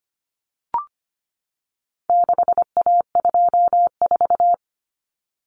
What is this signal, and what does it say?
Morse code, heard as a keyed tone, about 25 wpm, 700 Hz. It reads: 6A24
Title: Common Cut Numbers - 4-Digit 25wpm